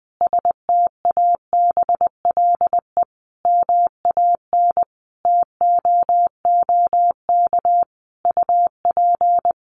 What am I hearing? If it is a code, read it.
Morse code, heard as a keyed tone, about 20 wpm, 700 Hz. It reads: STABLE MAN TOOK UP
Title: Sentences from Top 1500 Words 20wpm